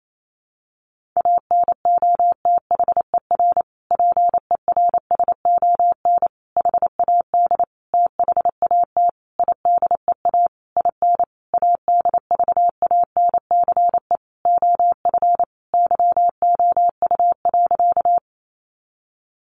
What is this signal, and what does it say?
Morse code, heard as a keyed tone, about 28 wpm, 700 Hz. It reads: ANOTHER PERSON HAD THAT IDEA IN ADVANCE OF YOU.
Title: Sentences from Top 1400 Words 28wpm